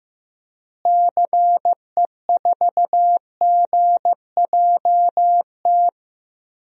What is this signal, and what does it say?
Morse code, heard as a keyed tone, about 15 wpm, 700 Hz. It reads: CE4GJT
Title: DXCC-100 Call Signs - 6 Characters - Repeated 3x - 15wpm